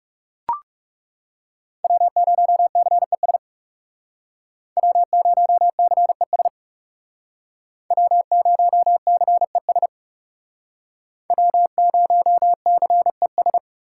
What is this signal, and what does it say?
Morse code, heard as a keyed tone, about 38 wpm, 700 Hz. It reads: W0CES W0CES W0CES W0CES